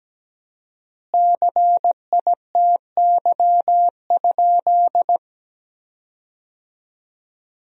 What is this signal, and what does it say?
Morse code, heard as a keyed tone, about 17 wpm, 700 Hz. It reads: CITY?
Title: 4x Word-Spacing - Sentences from Top 300 Words - 17wpm